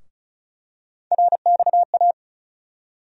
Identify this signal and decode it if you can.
Morse code, heard as a keyed tone, about 34 wpm, 700 Hz. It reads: RXA
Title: USA Call Sign Postfixes - 3 Letters 35wpm